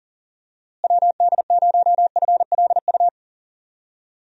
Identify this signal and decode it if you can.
Morse code, heard as a keyed tone, about 39 wpm, 700 Hz. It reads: WD0FLU
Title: USA Call Signs - 6 Characters - Repeated 3x - 40wpm